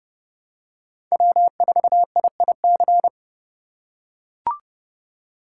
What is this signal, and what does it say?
Morse code, heard as a keyed tone, about 30 wpm, 700 Hz. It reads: W4IIC
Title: K1USN SST QSO Practice - 30wpm